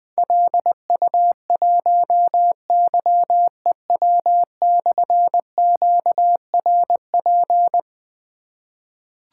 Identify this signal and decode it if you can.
Morse code, heard as a keyed tone, about 20 wpm, 700 Hz. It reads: LU1YEW/QRP
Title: Difficult Call Signs 20wpm